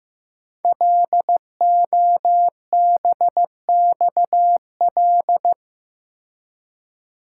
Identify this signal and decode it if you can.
Morse code, heard as a keyed tone, about 15 wpm, 700 Hz. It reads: LOBXL
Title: Code Groups - 5 Letters - 15wpm